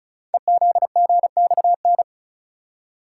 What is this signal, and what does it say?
Morse code, heard as a keyed tone, about 36 wpm, 690 Hz. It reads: EZGXN
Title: Code Groups - 5 Letter:Numbers - 35wpm